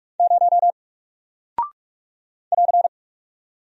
Morse code, heard as a keyed tone, about 44 wpm, 700 Hz. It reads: 0 <AR>
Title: Single Character (Letter:Number:Punctuation:Prosign) Alt 45wpm